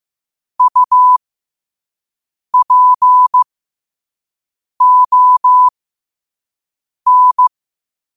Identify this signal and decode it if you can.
Morse code, heard as a keyed tone, about 15 wpm, spaced as for 5 wpm, 1000 Hz. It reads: UPON
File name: qotdfiles / 5wpm